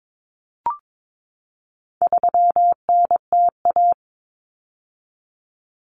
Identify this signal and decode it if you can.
Morse code, heard as a keyed tone, about 22 wpm, 700 Hz. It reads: 3NTA
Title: Common Cut Numbers - 4-Digit 22wpm